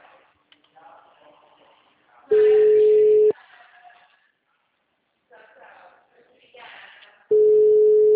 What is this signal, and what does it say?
Ringback tone (Europe, 425 Hz), with speech.
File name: out- -2013- - - .32463